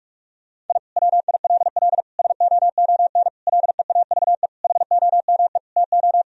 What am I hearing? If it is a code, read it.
Morse code, heard as a keyed tone, about 46 wpm, 700 Hz. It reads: I WILL SOON LEAVE HOME TO
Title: Sentences from Top 1000 Words - Review Part 2 - 45wpm